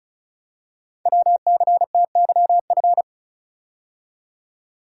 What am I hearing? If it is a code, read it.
Morse code, heard as a keyed tone, about 36 wpm, 690 Hz. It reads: WCTYF
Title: Code Groups - 5 Letter:Numbers - 35wpm